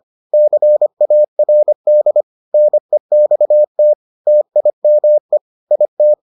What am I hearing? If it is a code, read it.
Morse code, heard as a keyed tone, about 25 wpm, 600 Hz. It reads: CARD NEXT TIME IT